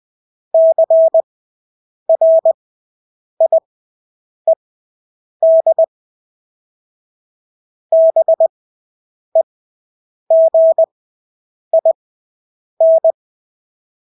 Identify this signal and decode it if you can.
Morse code, heard as a keyed tone, about 20 wpm, spaced as for 8 wpm, 650 Hz. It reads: CRIED BEGIN
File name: Engelske ord 20/8 wpm